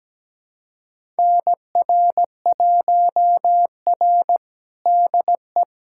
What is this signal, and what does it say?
Morse code, heard as a keyed tone, about 17 wpm, 700 Hz. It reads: NR1R DE
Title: USA Callsign DE Callsign - 4 Characters - 17wpm